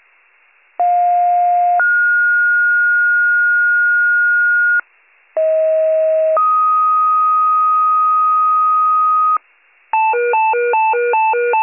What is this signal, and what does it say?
Two-tone page, heard as a one-second tone, then a three-second tone, about 710 then 1430 Hz, about 640 then 1250 Hz, with noise. It is heard from 0.8 to 9.4 s.